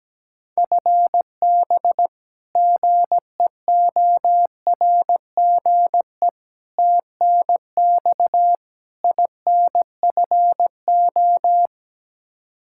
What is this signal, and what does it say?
Morse code, heard as a keyed tone, about 17 wpm, 700 Hz. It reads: FB GEORGE TNX INFO
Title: K1USN SST QSO Practice 17wpm